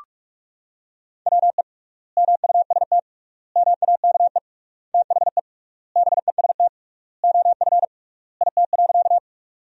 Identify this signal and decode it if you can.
Morse code, heard as a keyed tone, about 46 wpm, 700 Hz. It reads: WE MUST MAKE THE BEST OF IT.
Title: 3x Word-Spacing - Sentences from Top 400 Words - 45wpm